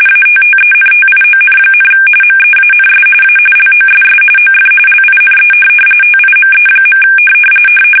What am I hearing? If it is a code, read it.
RTTY, about 45.45 baud. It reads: ATTENTION, MEANS A LOSS OF LINE TIME.
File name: fsk850